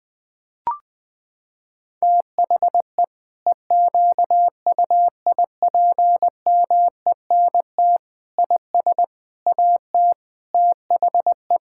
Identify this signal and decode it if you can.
Morse code, heard as a keyed tone, about 20 wpm, 700 Hz. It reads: THE EQUIPMENT IS AT THE
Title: Sentences from Top 1000 Words - Review Part 5 - 20wpm